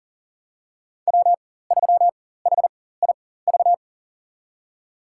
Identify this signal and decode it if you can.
Morse code, heard as a keyed tone, about 40 wpm, spaced as for 19 wpm, 700 Hz. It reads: W3HIV
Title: USA Call Signs - 5 Characters - 4x Character Spacing - 40wpm